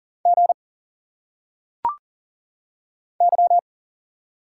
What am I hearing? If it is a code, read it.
Morse code, heard as a keyed tone, about 40 wpm, 700 Hz. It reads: G Y
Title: Single Letters - Phonetic Pronunciation 40wpm